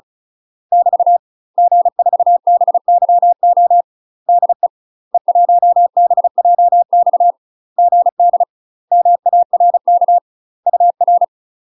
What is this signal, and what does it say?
Morse code, heard as a keyed tone, about 34 wpm, 700 Hz. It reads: <BT> G4BYO DE E1BJX GD MARK UR
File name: UK MASTER QSOs TRAINING / 35WPM / UK MASTER QSOs 0021